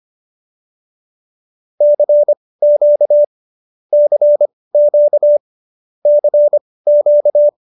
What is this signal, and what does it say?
Morse code, heard as a keyed tone, about 25 wpm, spaced as for 18 wpm, 590 Hz. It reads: CQ CQ CQ
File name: qso222 18